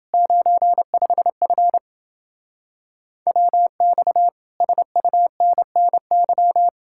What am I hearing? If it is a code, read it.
Morse code, heard as a keyed tone, about 28 wpm, 700 Hz. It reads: 95F WX SUNNY